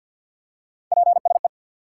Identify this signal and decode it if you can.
Morse code, heard as a keyed tone, about 52 wpm, 700 Hz. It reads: PSE
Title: Top 180 QSO Elements 50wpm